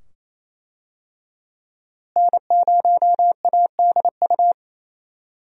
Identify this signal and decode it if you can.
Morse code, heard as a keyed tone, about 28 wpm, 700 Hz. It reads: N0ADU
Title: USA Call Signs - 5 Characters - Repeated 2x - 28wpm